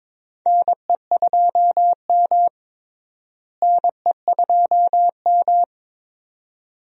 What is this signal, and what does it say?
Morse code, heard as a keyed tone, about 21 wpm, 700 Hz. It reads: NE2M NE2M
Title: USA Call Signs - 4 Characters - Repeated 2x - 22wpm